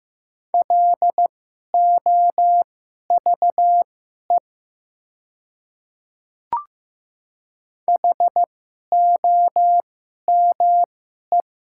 Morse code, heard as a keyed tone, about 15 wpm, spaced as for 11 wpm, 700 Hz. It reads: LOVE HOME
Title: 2x Character Spacing - Top 500 Words - 15wpm